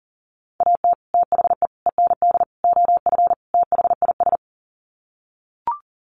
Morse code, heard as a keyed tone, about 40 wpm, 700 Hz. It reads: AT THE END OF THIS
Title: Sets of 5 Words 40wpm